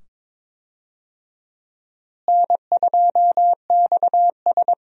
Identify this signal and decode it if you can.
Morse code, heard as a keyed tone, about 22 wpm, 700 Hz. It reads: N2XS
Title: USA Call Signs - 4 Characters - Repeated 2x - 22wpm